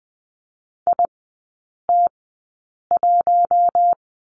Morse code, heard as a keyed tone, about 20 wpm, spaced as for 8 wpm, 700 Hz. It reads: IT1
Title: 133 Lesson 22e - Question Mark - Three Characters 20wpm